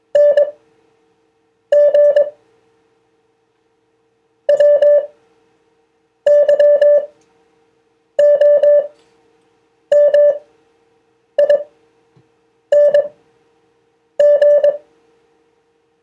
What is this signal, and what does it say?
Morse code, heard as a keyed tone, about 21 wpm, spaced as for 6 wpm, 580 Hz. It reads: NG WYOMING